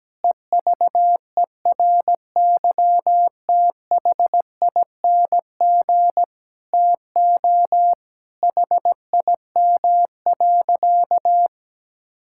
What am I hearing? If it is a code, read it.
Morse code, heard as a keyed tone, about 17 wpm, 700 Hz. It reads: EVERYTHING TO HIM.
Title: Sentences from Top 1600 Words 17wpm